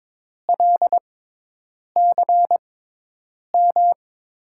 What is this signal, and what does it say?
Morse code, heard as a keyed tone, about 22 wpm, spaced as for 8 wpm, 700 Hz. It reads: LCM
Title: USA Call Signs - 6 Characters - 6x Character Spacing - 22wpm